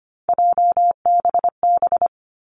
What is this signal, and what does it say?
Morse code, heard as a keyed tone, about 25 wpm, 700 Hz. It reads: JBB
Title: Three Challenging Letter-Number Combinations 25wpm